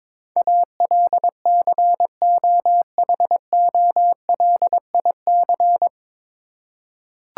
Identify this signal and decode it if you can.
Morse code, heard as a keyed tone, about 22 wpm, 700 Hz. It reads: ALCOHOLIC
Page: Morse Code Ninja practice 22wpm